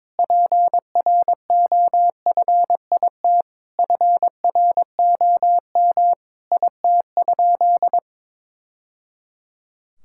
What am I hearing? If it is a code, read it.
Morse code, heard as a keyed tone, about 22 wpm, 700 Hz. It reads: PROFIT FROM IT?
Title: Sentences from Top 1000 Words - Review Part 5 - 22wpm